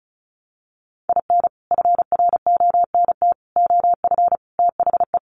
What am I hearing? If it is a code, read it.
Morse code, heard as a keyed tone, about 34 wpm, 700 Hz. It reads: IN FRONT OF THE
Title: Sets of 5 Words Encore2 35wpm